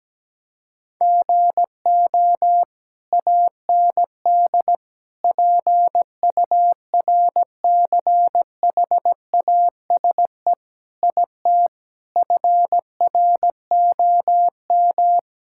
Morse code, heard as a keyed tone, about 17 wpm, 700 Hz. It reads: GO AND PURCHASE IT FROM
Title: Sentences from Top 1000 Words - Review Part 5 - 17wpm